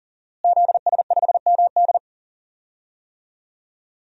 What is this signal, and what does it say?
Morse code, heard as a keyed tone, about 40 wpm, 700 Hz. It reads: ZS5MD
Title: DXCC-100 Call Signs - 5 Characters - Repeated 2x - 40wpm